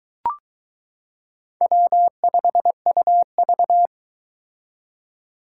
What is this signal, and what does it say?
Morse code, heard as a keyed tone, about 22 wpm, 700 Hz. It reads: W5UV